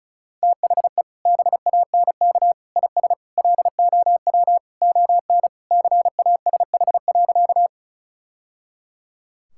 Morse code, heard as a keyed tone, about 36 wpm, 700 Hz. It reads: THE BANK IS LOW ON CASH.
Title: Sentences from Top 1000 Words - Review Part 5 - 35wpm